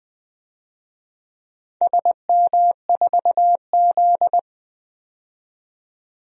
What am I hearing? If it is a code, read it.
Morse code, heard as a keyed tone, about 21 wpm, 700 Hz. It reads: SM4Z
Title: DXCC-100 Call Signs - 4 Characters - 20wpm